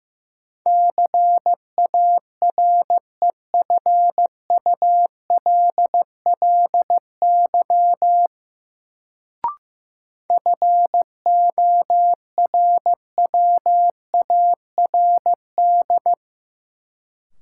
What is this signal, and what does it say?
Morse code, heard as a keyed tone, about 16 wpm, 700 Hz. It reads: CAREFULLY FORWARD
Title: Sentences from Top 1300 Words - New Word Drill - 15wpm